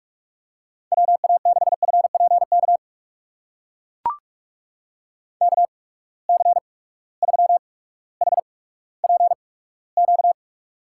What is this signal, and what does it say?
Morse code, heard as a keyed tone, about 42 wpm, spaced as for 19 wpm, 700 Hz. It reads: WA6FPK KC3HPQ
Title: USA Call Signs - 6 Characters - 8x Character Spacing - 45wpm